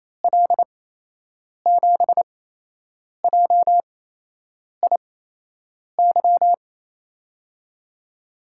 Morse code, heard as a keyed tone, about 28 wpm, spaced as for 8 wpm, 700 Hz. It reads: L7JIY
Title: USA Call Signs - 6 Characters - 8x Character Spacing - 28wpm